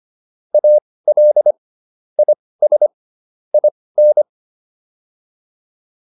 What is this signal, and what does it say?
Morse code, heard as a keyed tone, about 25 wpm, spaced as for 18 wpm, 600 Hz. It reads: AL IS IN